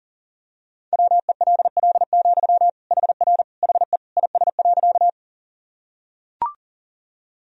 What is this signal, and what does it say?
Morse code, heard as a keyed tone, about 40 wpm, 700 Hz. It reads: WELL, HR HE IS.
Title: Sentences from Top 200 Words and Abbreviations 40wpm